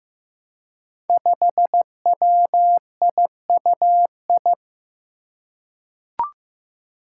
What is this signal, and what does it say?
Morse code, heard as a keyed tone, about 15 wpm, 700 Hz. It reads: 5WIUI
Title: Five Challenging Letter-Number Combinations 15wpm